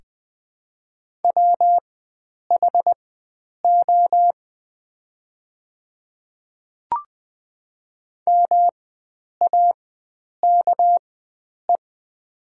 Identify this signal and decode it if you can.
Morse code, heard as a keyed tone, about 20 wpm, spaced as for 9 wpm, 700 Hz. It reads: WHO MAKE
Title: 4x Character Spacing - Top 100 Words - 20wpm